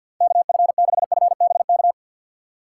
Morse code, heard as a keyed tone, about 65 wpm, 700 Hz. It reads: KF6FBK